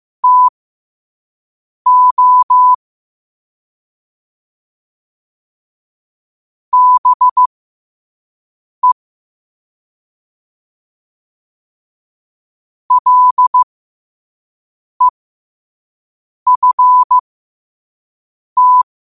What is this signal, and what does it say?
Morse code, heard as a keyed tone, about 15 wpm, spaced as for 5 wpm, 1000 Hz. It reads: TO BE LEFT
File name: 5wpm